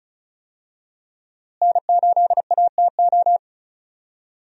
Morse code, heard as a keyed tone, about 36 wpm, 700 Hz. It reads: N8ATO
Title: USA Call Signs - 5 Characters - Repeated 3x - 35wpm